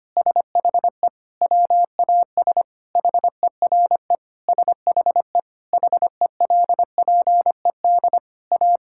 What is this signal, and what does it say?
Morse code, heard as a keyed tone, about 25 wpm, 700 Hz. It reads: SHE WAS HERE SHE HELPED A
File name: practice / 25wpm / Sentences with 3-Word Phrases - 25wpm